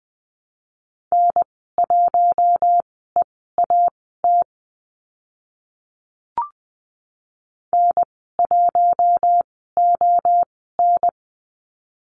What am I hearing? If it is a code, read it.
Morse code, heard as a keyed tone, about 20 wpm, spaced as for 14 wpm, 700 Hz. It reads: N1EAT N1ON
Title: 042 Lesson 8g - Number 1 - Call Signs 20wpm